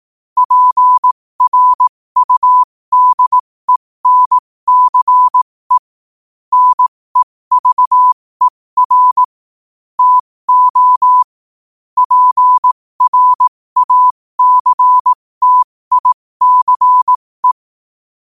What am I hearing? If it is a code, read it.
Morse code, heard as a keyed tone, about 18 wpm, spaced as for 15 wpm, 1000 Hz. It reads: PRUDENCE NEVER TO PRACTICE
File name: qotdfiles / 15wpm